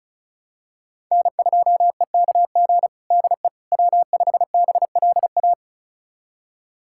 Morse code, heard as a keyed tone, about 36 wpm, 700 Hz. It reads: N2EKG DE W5BLA
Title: USA Callsign DE Callsign - 5 Characters - 35wpm